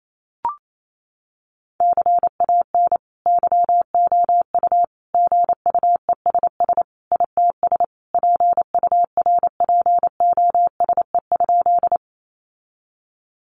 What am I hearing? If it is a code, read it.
Morse code, heard as a keyed tone, about 28 wpm, 700 Hz. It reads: CAN YOU GUESS ITS PURPOSE?
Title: Sentences from Top 1500 Words 28wpm